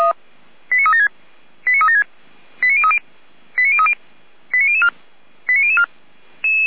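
Five-tone selective call (ZVEI), heard as quick runs of five tones, tones of about 80 ms. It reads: 89376 89376 89030 89030 89004 89004
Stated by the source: ZVEI five-tone selective call received on an SDR, from the Feuerwehr Orlamünde server.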